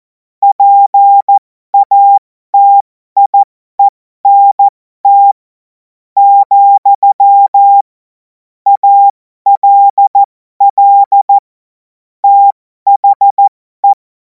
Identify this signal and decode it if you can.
Morse code, heard as a keyed tone, about 14 wpm, spaced as for 11 wpm, 800 Hz. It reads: PATIENT, ALL THE
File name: Morsekurs / Witze